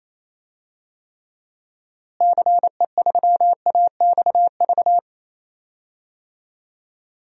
Morse code, heard as a keyed tone, about 28 wpm, 700 Hz. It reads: CE3AXV
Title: DXCC-100 Call Signs - 6 Characters - Repeated 2x - 28wpm